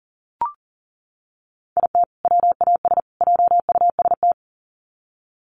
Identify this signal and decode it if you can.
Morse code, heard as a keyed tone, about 40 wpm, 700 Hz. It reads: IT WAS JUST
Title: Sets of 3 Words 40wpm